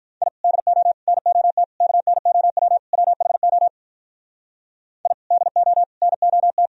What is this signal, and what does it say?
Morse code, heard as a keyed tone, about 44 wpm, 700 Hz. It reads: I DO NOT KNOW WHO I DO NOT